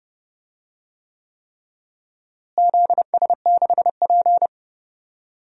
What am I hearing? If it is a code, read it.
Morse code, heard as a keyed tone, about 32 wpm, 700 Hz. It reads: ZS6P
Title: DXCC-100 Call Signs - 4 Characters - Repeated 3x - 30wpm